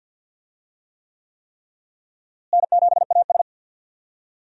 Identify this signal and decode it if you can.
Morse code, heard as a keyed tone, about 51 wpm, 700 Hz. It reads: N7AS
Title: USA Call Signs - 4 Characters - Repeated 3x - 50wpm